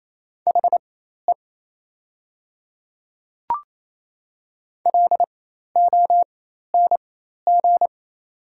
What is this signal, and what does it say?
Morse code, heard as a keyed tone, about 28 wpm, spaced as for 13 wpm, 690 Hz. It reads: HE LONG
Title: 4x Character Spacing - Top 300 Words - 28wpm